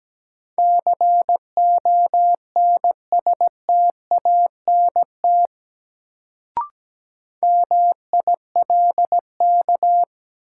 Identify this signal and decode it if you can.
Morse code, heard as a keyed tone, about 17 wpm, 700 Hz. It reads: CONSTANT MILK
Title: Sentences from Top 1500 Words - New Word Drill - 17wpm